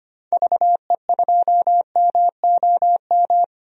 Morse code, heard as a keyed tone, about 25 wpm, 700 Hz. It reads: VE2MOM
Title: DXCC-100 Call Signs - 6 Characters - 25wpm